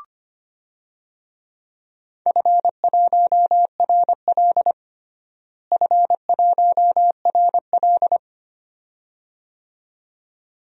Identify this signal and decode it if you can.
Morse code, heard as a keyed tone, about 25 wpm, 700 Hz. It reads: F1RL F1RL
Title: DXCC-100 Call Signs - 4 Characters - Repeated 2x - 25wpm